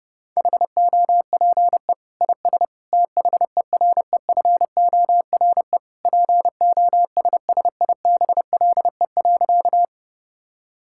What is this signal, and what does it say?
Morse code, heard as a keyed tone, about 30 wpm, 700 Hz. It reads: HOPE IS THEREFORE POSSIBLE.
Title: Sentences from Top 1000 Words - Review Part 5 - 30wpm